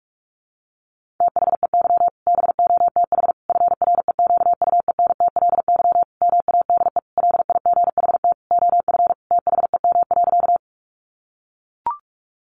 Morse code, heard as a keyed tone, about 44 wpm, 700 Hz. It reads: THEY BOTH FREQUENTLY MADE LIGHT OF THEM.
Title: Sentences from Top 1300 Words 45wpm